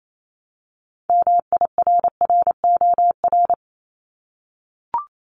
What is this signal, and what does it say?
Morse code, heard as a keyed tone, about 28 wpm, 700 Hz. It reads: MIRROR
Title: Multi-letter Phonemes 28wpm